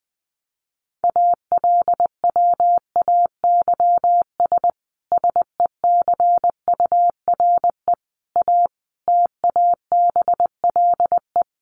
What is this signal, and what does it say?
Morse code, heard as a keyed tone, about 20 wpm, 700 Hz. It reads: ALWAYS SECURE A TABLE
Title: Sentences from Top 1200 Words 20wpm